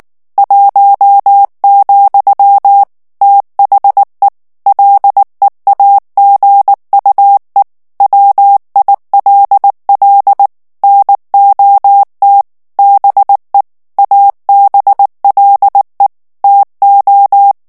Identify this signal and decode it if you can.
Morse code, heard as a keyed tone, about 19 wpm, 800 Hz. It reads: 1, THE LEAGUE WILL NOT BE ABLE TO
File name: ANGLAIS texte clair 3 vitesse 18 mots minute